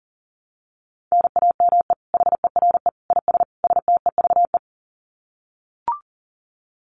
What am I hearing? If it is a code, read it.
Morse code, heard as a keyed tone, about 40 wpm, 700 Hz. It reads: NAME HERE IS STEVE
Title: Common QSO Phrases 40wpm